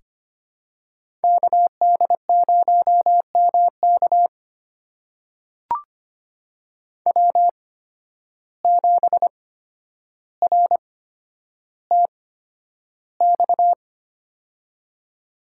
Morse code, heard as a keyed tone, about 25 wpm, spaced as for 11 wpm, 700 Hz. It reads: KD0MK W7RTX
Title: USA Call Signs - 5 Characters - 8x Character Spacing - 25wpm